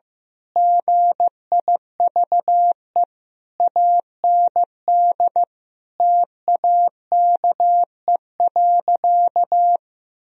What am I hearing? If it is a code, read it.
Morse code, heard as a keyed tone, about 14 wpm, 700 Hz. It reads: GIVE AND TAKE.
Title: Sentences from Top 1000 Words - Review Part 4 - 15wpm